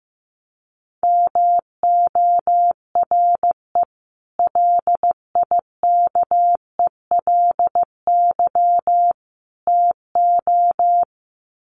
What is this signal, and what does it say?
Morse code, heard as a keyed tone, about 15 wpm, 700 Hz. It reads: MORE LIKELY TO
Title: Sets of 3 Words 15wpm